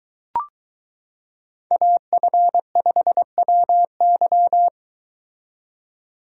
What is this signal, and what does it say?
Morse code, heard as a keyed tone, about 22 wpm, 700 Hz. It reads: AF5WY